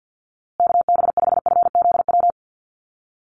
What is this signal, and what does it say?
Morse code, heard as a keyed tone, about 50 wpm, 700 Hz. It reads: KB5FZW
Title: USA Call Signs Repeated 3x 50wpm